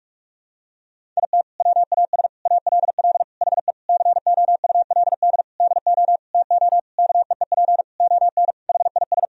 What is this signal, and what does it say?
Morse code, heard as a keyed tone, about 46 wpm, 700 Hz. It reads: IT WAS ALL HE COULD DO TO KEEP ON HIS
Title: Sentences from Top 1000 Words - Review Part 1 - 45wpm